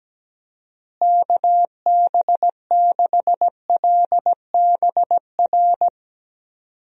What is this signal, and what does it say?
Morse code, heard as a keyed tone, about 17 wpm, 700 Hz. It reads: KB6LBR
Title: USA Call Signs - 6 Characters - Repeated 2x - 17wpm